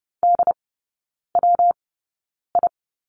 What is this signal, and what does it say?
Morse code, heard as a keyed tone, about 30 wpm, spaced as for 9 wpm, 700 Hz. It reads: DWI
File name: Lesson 19e - Letter w - Three Characters 30wpm